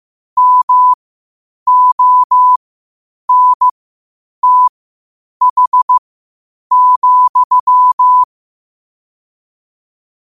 Morse code, heard as a keyed tone, about 15 wpm, spaced as for 8 wpm, 1000 Hz. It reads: MONTH,
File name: qotdfiles / 8wpm